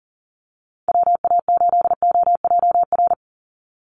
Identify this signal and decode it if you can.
Morse code, heard as a keyed tone, about 39 wpm, 700 Hz. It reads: WA8OJR
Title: Call Signs Repeated 2x - 40wpm